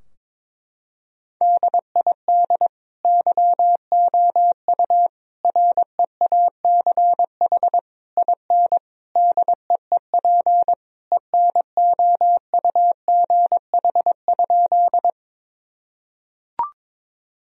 Morse code, heard as a keyed tone, about 22 wpm, 700 Hz. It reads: DID YOU REACH IN DEEP ENOUGH?
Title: Sentences from Top 1000 Words - Review Part 3 - 22wpm